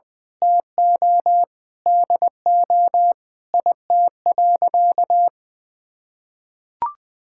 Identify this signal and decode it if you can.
Morse code, heard as a keyed tone, about 21 wpm, 700 Hz. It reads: TO DO IT.
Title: Sentences from Top 1000 Words - Review Part 6 - 20wpm